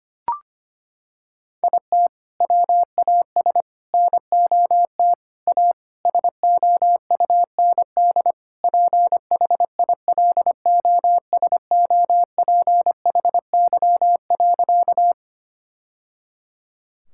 Morse code, heard as a keyed tone, about 25 wpm, 700 Hz. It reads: IT WAS NOT A SOUND PHILOSOPHY.
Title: Sentences from Top 1000 Words - Review Part 6 - 25wpm